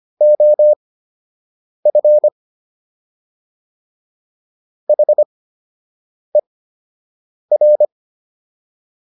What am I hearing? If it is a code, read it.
Morse code, heard as a keyed tone, about 25 wpm, spaced as for 7 wpm, 600 Hz. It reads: OF HER